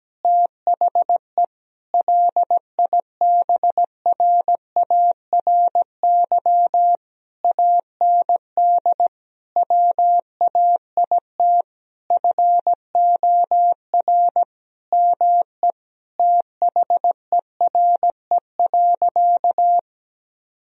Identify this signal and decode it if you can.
Morse code, heard as a keyed tone, about 17 wpm, 700 Hz. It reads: THE LIBRARY AND WAIT FOR ME THERE.
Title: Sentences from Top 1000 Words - Review Part 4 - 17wpm